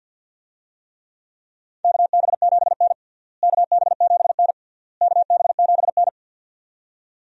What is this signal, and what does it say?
Morse code, heard as a keyed tone, about 52 wpm, 700 Hz. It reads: KB7N KB7N KB7N
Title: USA Call Signs - 4 Characters - Repeated 3x - 50wpm